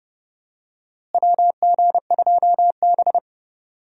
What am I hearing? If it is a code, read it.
Morse code, heard as a keyed tone, about 31 wpm, 700 Hz. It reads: WG2B
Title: USA Call Signs - 4 Characters - Repeated 2x - 30wpm